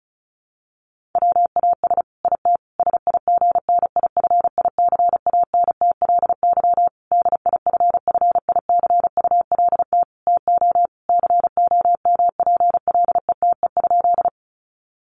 Morse code, heard as a keyed tone, about 34 wpm, 700 Hz. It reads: WAS IT SIGNIFICANTLY DIFFICULT TO COMPLETE?
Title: Sentences from Top 1500 Words 35wpm